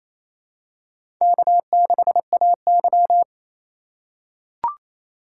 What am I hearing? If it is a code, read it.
Morse code, heard as a keyed tone, about 28 wpm, 700 Hz. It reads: K6AY
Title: USA Call Signs - 4 Characters - 2x Character Spacing - 28wpm